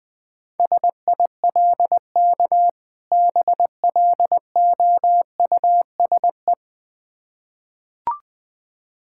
Morse code, heard as a keyed tone, about 20 wpm, 690 Hz. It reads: SILK BLOUSE
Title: Difficult Words with Hints - 20wpm